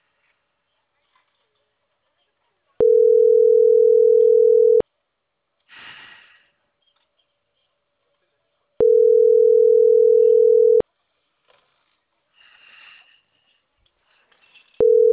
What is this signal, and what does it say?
Ringback tone (North America), 440 + 480 Hz.